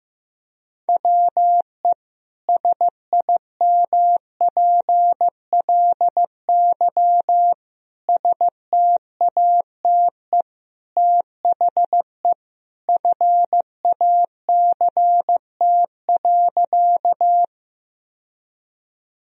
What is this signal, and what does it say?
Morse code, heard as a keyed tone, about 15 wpm, 700 Hz. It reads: WE SIMPLY STATE THE FACT.
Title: Sentences from Top 1000 Words - Review Part 4 - 15wpm